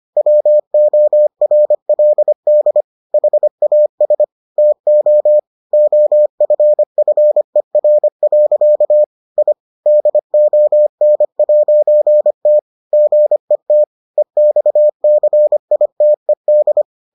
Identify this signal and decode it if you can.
Morse code, heard as a keyed tone, about 25 wpm, 600 Hz. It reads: WORLD HAS TO OFFER. I DON'T GET EXCITED